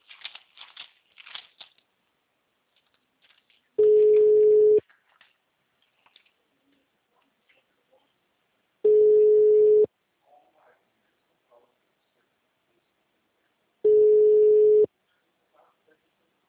Ringback tone (Europe, 425 Hz).